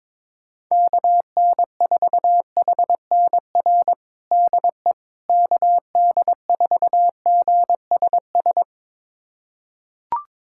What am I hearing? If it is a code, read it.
Morse code, heard as a keyed tone, about 22 wpm, 700 Hz. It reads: KN4HNR DE KD4GSS
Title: USA Callsign DE Callsign - 6 Characters - 22wpm